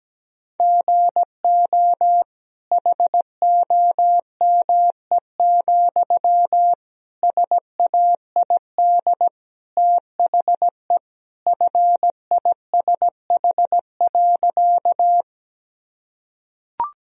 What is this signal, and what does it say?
Morse code, heard as a keyed tone, about 17 wpm, 700 Hz. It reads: GO HOME, SAID THE FISH.
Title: Sentences from Top 1000 Words - Review Part 1 - 17wpm